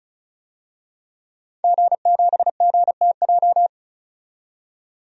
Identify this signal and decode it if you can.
Morse code, heard as a keyed tone, about 34 wpm, 700 Hz. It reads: G7GTJ
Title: DXCC-100 Call Signs - 5 Characters - 35wpm